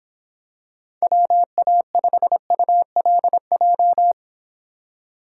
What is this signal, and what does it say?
Morse code, heard as a keyed tone, about 26 wpm, 700 Hz. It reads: WA5ULJ